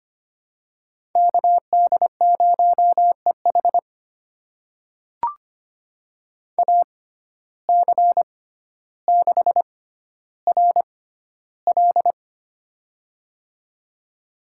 Morse code, heard as a keyed tone, about 25 wpm, spaced as for 13 wpm, 700 Hz. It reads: KD0EH AC6RL
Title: USA Call Signs - 5 Characters - 6x Character Spacing - 25wpm